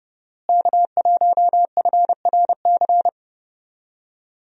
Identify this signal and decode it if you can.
Morse code, heard as a keyed tone, about 30 wpm, 700 Hz. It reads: K1FRC
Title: USA Call Signs - 5 Characters - Repeated 2x - 30wpm